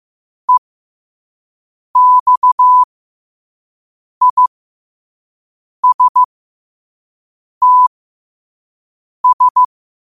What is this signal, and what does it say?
Morse code, heard as a keyed tone, about 15 wpm, spaced as for 5 wpm, 1000 Hz. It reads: EXISTS
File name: qotdfiles / 5wpm